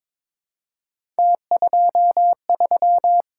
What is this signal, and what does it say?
Morse code, heard as a keyed tone, about 23 wpm, 700 Hz. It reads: T23
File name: practice / 22wpm / Three Letter-Number Combinations - 22wpm